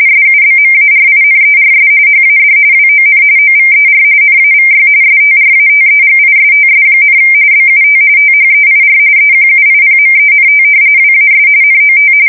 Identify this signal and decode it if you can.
RTTY, about 45.45 baud. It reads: SUBMARINERS HAD WON OVER NAVY STRATEGISTS, BUT THEIR EQUIPMENT WAS NOT
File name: fsk170 / usnavy4